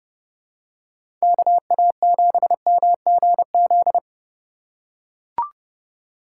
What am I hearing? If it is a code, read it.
Morse code, heard as a keyed tone, about 30 wpm, 700 Hz. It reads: KA7MGZ